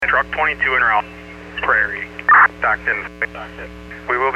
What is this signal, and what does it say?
Emergency Alert System (SAME) tones, heard as harsh buzzing bursts, with speech. It is heard from 0.4 to 3.3 s.